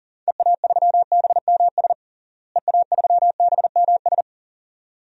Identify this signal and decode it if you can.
Morse code, heard as a keyed tone, about 40 wpm, 700 Hz. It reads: EA3BMS EA3BMS
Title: DXCC-100 Call Signs - 6 Characters - Repeated 2x - 40wpm